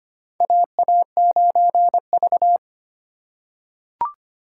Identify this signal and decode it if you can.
Morse code, heard as a keyed tone, about 25 wpm, 700 Hz. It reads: AA9V
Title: USA Call Signs - 4 Characters - 2x Character Spacing - 25wpm